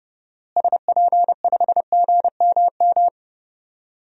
Morse code, heard as a keyed tone, about 30 wpm, 700 Hz. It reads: SP5GMM
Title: DXCC-100 Call Signs - 6 Characters - Repeated 2x - 30wpm